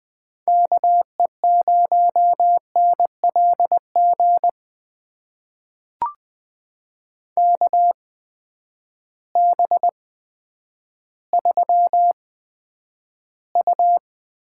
Morse code, heard as a keyed tone, about 20 wpm, spaced as for 10 wpm, 700 Hz. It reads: KE0NLG KB3U
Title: USA Call Signs - 6 Characters - 8x Character Spacing - 20wpm